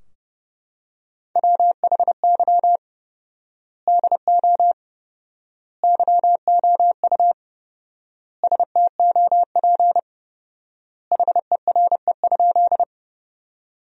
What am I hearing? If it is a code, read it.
Morse code, heard as a keyed tone, about 30 wpm, 700 Hz. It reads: WHY DO YOU STOP HERE?
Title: 4x Word-Spacing - Sentences from Top 300 Words - 30wpm